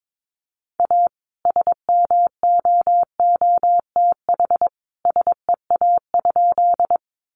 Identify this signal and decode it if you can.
Morse code, heard as a keyed tone, about 22 wpm, 700 Hz. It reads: A SMOOTH SEA?
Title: Sentences from Top 1400 Words 22wpm